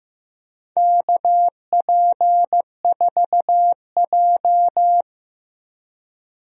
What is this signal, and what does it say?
Morse code, heard as a keyed tone, about 15 wpm, 700 Hz. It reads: KP4J
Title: DXCC-100 Call Signs - 4 Characters - Repeated 2x - 15wpm